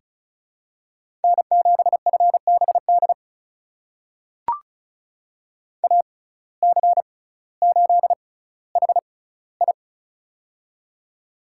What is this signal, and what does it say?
Morse code, heard as a keyed tone, about 34 wpm, spaced as for 18 wpm, 700 Hz. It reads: N7FBD AC8HI
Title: USA Call Signs - 5 Characters - 6x Character Spacing - 35wpm